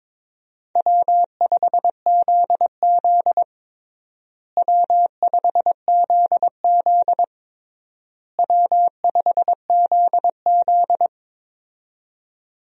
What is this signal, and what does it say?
Morse code, heard as a keyed tone, about 22 wpm, 700 Hz. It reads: W5ZZ W5ZZ W5ZZ
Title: USA Call Signs - 4 Characters - Repeated 3x - 22wpm